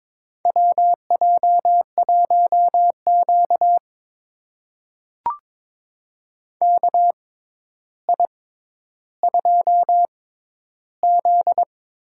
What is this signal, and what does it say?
Morse code, heard as a keyed tone, about 22 wpm, spaced as for 11 wpm, 700 Hz. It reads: WJ1Q KI2Z
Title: USA Call Signs - 4 Characters - 6x Character Spacing - 22wpm